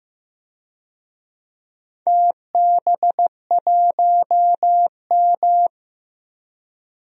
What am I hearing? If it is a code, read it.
Morse code, heard as a keyed tone, about 15 wpm, 700 Hz. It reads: TB1M
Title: DXCC-100 Call Signs - 4 Characters - 15wpm